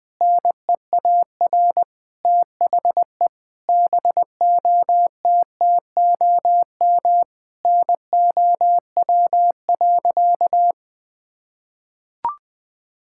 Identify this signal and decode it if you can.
Morse code, heard as a keyed tone, about 20 wpm, 700 Hz. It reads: NEAR THE BOTTOM NOW.
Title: Sentences from Top 1000 Words - Review Part 6 - 20wpm